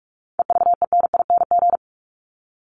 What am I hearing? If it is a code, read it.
Morse code, heard as a keyed tone, about 44 wpm, 700 Hz. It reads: EVENING
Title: Top 1501-1750 Words 45wpm